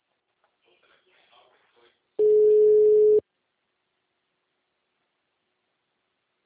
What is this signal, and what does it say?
Ringback tone (Europe, 425 Hz), with speech.